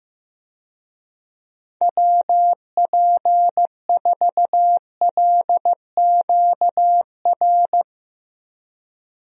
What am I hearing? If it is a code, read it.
Morse code, heard as a keyed tone, about 14 wpm, 700 Hz. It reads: WP4LQR
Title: USA Call Signs - 6 Characters - Repeated 3x - 15wpm